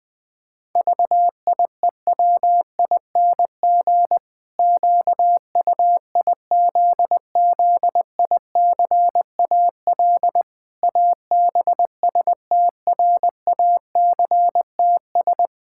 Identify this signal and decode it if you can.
Morse code, heard as a keyed tone, about 20 wpm, 700 Hz. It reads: VIEWING QUIZZICAL ABSTRACTS
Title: Pangram Challenge v2 - 20wpm